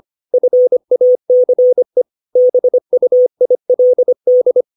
Morse code, heard as a keyed tone, about 25 wpm, 500 Hz. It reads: FACE BUILD
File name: 632 words / 632 Words 25WPM